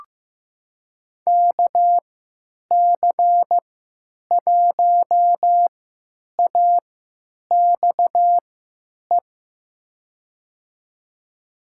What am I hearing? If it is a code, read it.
Morse code, heard as a keyed tone, about 16 wpm, spaced as for 9 wpm, 700 Hz. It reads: KC1AXE
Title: USA Call Signs - 6 Characters - 3x Character Spacing - 15wpm